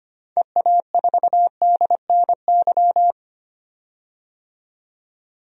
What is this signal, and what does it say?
Morse code, heard as a keyed tone, about 25 wpm, 700 Hz. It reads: EA4DNY
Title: DXCC-100 Call Signs - 6 Characters - 25wpm